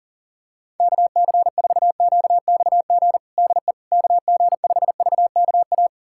Morse code, heard as a keyed tone, about 37 wpm, 700 Hz. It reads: KC4QXG DE KG5VKA
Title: USA Callsign DE Callsign - 6 Characters - 40wpm